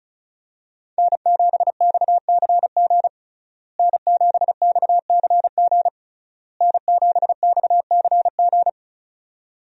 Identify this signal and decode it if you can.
Morse code, heard as a keyed tone, about 36 wpm, 700 Hz. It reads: N7XCG N7XCG N7XCG
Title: USA Call Signs - 5 Characters - Repeated 3x - 35wpm